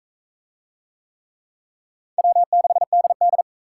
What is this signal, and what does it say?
Morse code, heard as a keyed tone, about 44 wpm, 700 Hz. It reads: W6DD